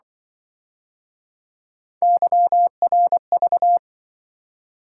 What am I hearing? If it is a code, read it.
Morse code, heard as a keyed tone, about 25 wpm, 700 Hz. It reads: YRV